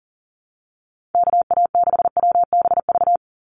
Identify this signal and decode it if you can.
Morse code, heard as a keyed tone, about 41 wpm, 700 Hz. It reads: KA6WBV
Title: USA Call Signs 40wpm